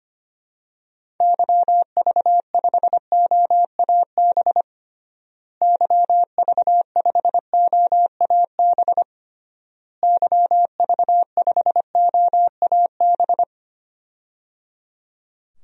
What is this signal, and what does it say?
Morse code, heard as a keyed tone, about 25 wpm, 700 Hz. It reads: YV5OAB YV5OAB YV5OAB
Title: DXCC-100 Call Signs - 6 Characters - Repeated 3x - 25wpm